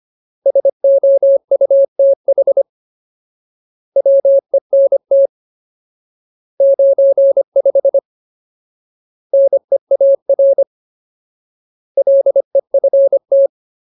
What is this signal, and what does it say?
Morse code, heard as a keyed tone, about 25 wpm, 550 Hz. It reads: SOUTH WENT 95 NEAR LEFT
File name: Ultimate Word List / 25WPM Ultimate